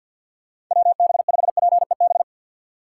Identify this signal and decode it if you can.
Morse code, heard as a keyed tone, about 51 wpm, 700 Hz. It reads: WB5PEB
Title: USA Call Signs - 6 Characters - 6x Character Spacing - 50wpm